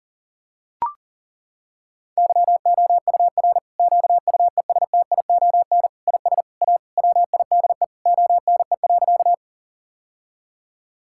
Morse code, heard as a keyed tone, about 37 wpm, 700 Hz. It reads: YOUR QUESTION IS A WIDE ONE.
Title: Sentences from Top 1000 Words - Review Part 6 - 40wpm